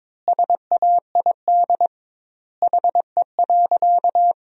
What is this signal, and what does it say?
Morse code, heard as a keyed tone, about 22 wpm, 700 Hz. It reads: SAID HE.
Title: 2x Word-Spacing - Sentences from Top 100 Words - 22wpm